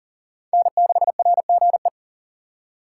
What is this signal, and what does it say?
Morse code, heard as a keyed tone, about 40 wpm, 700 Hz. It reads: N6RGE
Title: USA Call Signs - 5 Characters - 6x Character Spacing - 40wpm